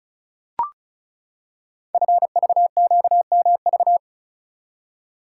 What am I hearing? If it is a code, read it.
Morse code, heard as a keyed tone, about 35 wpm, 700 Hz. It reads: FVQMV
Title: Code Groups - 5 Letter:Numbers - 35wpm